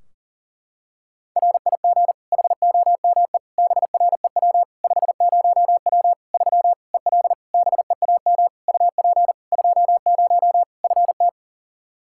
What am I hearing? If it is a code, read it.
Morse code, heard as a keyed tone, about 40 wpm, 700 Hz. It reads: RIG HOME BREW 50W 3 EL BEAM UP 20 FT
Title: Practice QSO - Exchange 2 - Info - 40wpm